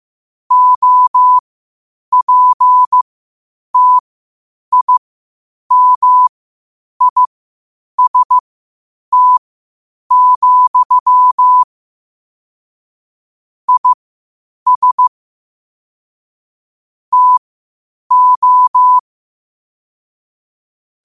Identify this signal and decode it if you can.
Morse code, heard as a keyed tone, about 15 wpm, spaced as for 8 wpm, 1000 Hz. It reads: OPTIMIST, IS TO
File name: qotdfiles / 8wpm